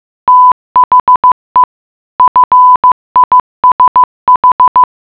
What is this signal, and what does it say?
Morse code, heard as a keyed tone, about 15 wpm, 1000 Hz. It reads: THE FISH
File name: mutation.15wpm